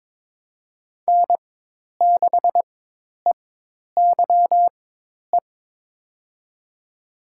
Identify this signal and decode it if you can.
Morse code, heard as a keyed tone, about 22 wpm, spaced as for 10 wpm, 700 Hz. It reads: N6EYE
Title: USA Call Signs - 4x Character Spacing - 22wpm